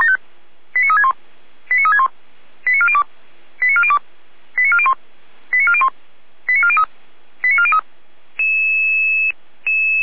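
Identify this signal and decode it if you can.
Five-tone selective call (ZVEI), heard as quick runs of five tones, tones of about 70 ms. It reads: 89361 89361 89402 89402 89401 89401 89403 89403
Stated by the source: ZVEI five-tone selective call received via SDR.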